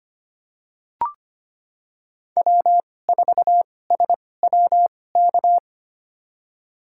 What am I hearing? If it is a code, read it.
Morse code, heard as a keyed tone, about 25 wpm, spaced as for 18 wpm, 700 Hz. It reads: W4SWK
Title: USA Call Signs - 5 Characters - 2x Character Spacing - 25wpm